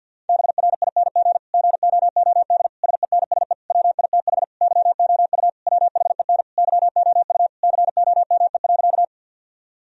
Morse code, heard as a keyed tone, about 50 wpm, 700 Hz. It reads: BRING GOOD SENSE WITH YOU WHEN YOU COME.
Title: Sentences from Top 1000 Words - Review Part 4 - 50wpm